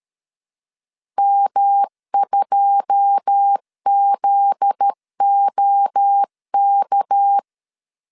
Morse code, heard as a keyed tone, about 13 wpm, 800 Hz. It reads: M2ZOK
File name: LEZIONI / 9 2 12wpm 2301